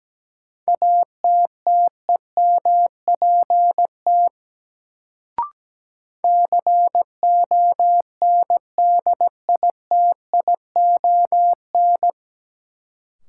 Morse code, heard as a keyed tone, about 17 wpm, 700 Hz. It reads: ATTEMPT CONDITION
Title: Sentences from Top 1000 Words - New Word Drill - 17wpm